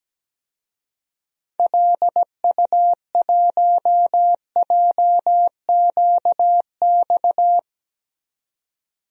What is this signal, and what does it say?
Morse code, heard as a keyed tone, about 17 wpm, 700 Hz. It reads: LU1JQX
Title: DXCC-100 Call Signs - 6 Characters - 17wpm